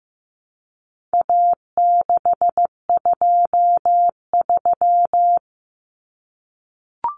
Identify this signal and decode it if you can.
Morse code, heard as a keyed tone, about 15 wpm, 700 Hz. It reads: A623
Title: Common Cut Numbers - 4-Digit 15wpm